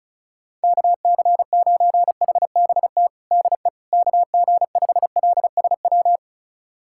Morse code, heard as a keyed tone, about 36 wpm, 700 Hz. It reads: KC9HBT DE KG5LSW
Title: USA Callsign DE Callsign 35wpm